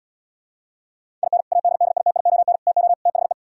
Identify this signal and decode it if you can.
Morse code, heard as a keyed tone, about 77 wpm, 700 Hz. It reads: IN ADDITION TO THE